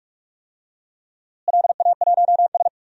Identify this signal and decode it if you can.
Morse code, heard as a keyed tone, about 44 wpm, 700 Hz. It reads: LA1S
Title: DXCC-100 Call Signs - 4 Characters - Repeated 2x - 45wpm